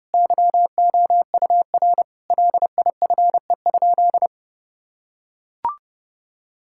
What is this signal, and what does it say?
Morse code, heard as a keyed tone, about 30 wpm, 700 Hz. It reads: YOUR LIFE?
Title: Sentences from Top 1000 Words - Review Part 6 - 30wpm